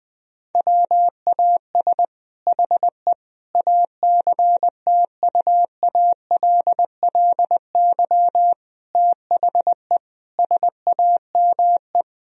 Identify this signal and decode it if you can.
Morse code, heard as a keyed tone, about 21 wpm, 700 Hz. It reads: WAS HE ACTUALLY THE SAME
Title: Sentences from Top 1000 Words - Review Part 4 - 20wpm